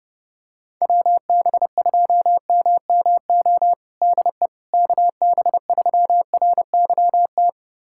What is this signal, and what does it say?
Morse code, heard as a keyed tone, about 30 wpm, 700 Hz. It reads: WB2MMO DE KB3RYT
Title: USA Callsign DE Callsign - 6 Characters - 30wpm